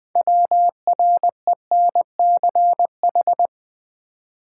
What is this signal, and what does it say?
Morse code, heard as a keyed tone, about 20 wpm, 700 Hz. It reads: WRENCH
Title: Easy Words with Hints - 20wpm